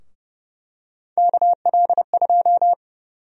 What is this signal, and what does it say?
Morse code, heard as a keyed tone, about 30 wpm, 700 Hz. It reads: KL2
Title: USA Call Sign Prefixes - 30wpm